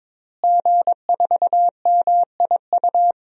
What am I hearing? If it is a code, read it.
Morse code, heard as a keyed tone, about 22 wpm, 700 Hz. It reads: G4MIU
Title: DXCC-100 Call Signs - 5 Characters - Repeated 2x - 22wpm